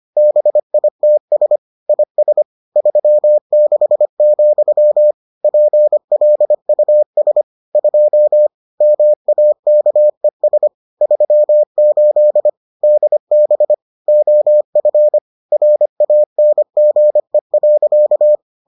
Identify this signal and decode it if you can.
Morse code, heard as a keyed tone, about 25 wpm, 600 Hz. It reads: BITS IS 36, PLUS 2 MAKES 38 DB OF RANGE.